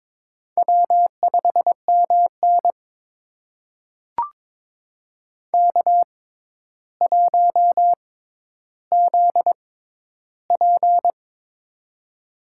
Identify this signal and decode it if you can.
Morse code, heard as a keyed tone, about 22 wpm, spaced as for 11 wpm, 700 Hz. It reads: W5MN K1ZP
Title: USA Call Signs - 4 Characters - 6x Character Spacing - 22wpm